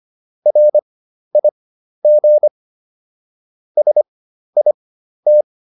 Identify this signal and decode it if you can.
Morse code, heard as a keyed tone, about 25 wpm, spaced as for 12 wpm, 600 Hz. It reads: RIG SIT